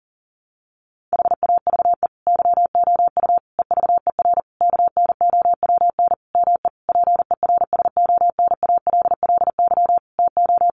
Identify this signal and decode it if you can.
Morse code, heard as a keyed tone, about 40 wpm, 700 Hz. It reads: HAVE YOU EVER KNOWN ME PERSONALLY TO
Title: Sentences from Top 1400 Words 40wpm